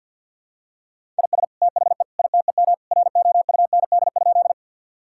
Morse code, heard as a keyed tone, about 47 wpm, 700 Hz. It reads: IS THE ITEM ROUND?
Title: Sentences from Top 1000 Words - Review Part 6 - 50wpm